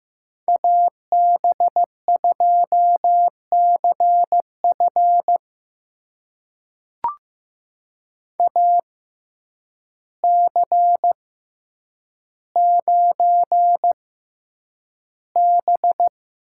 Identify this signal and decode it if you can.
Morse code, heard as a keyed tone, about 15 wpm, spaced as for 8 wpm, 700 Hz. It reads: AB2CF AC9B
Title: USA Call Signs - 5 Characters - 6x Character Spacing - 15wpm